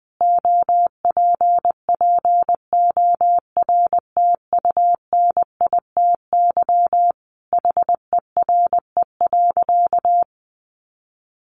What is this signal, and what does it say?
Morse code, heard as a keyed tone, about 20 wpm, 700 Hz. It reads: OPPORTUNITY HERE.
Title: Sentences from Top 1300 Words 20wpm